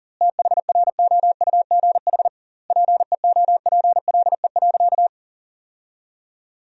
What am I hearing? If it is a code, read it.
Morse code, heard as a keyed tone, about 37 wpm, 700 Hz. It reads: THROUGH PEOPLE.
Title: 2x Word-Spacing - Sentences from Top 200 Words - 40wpm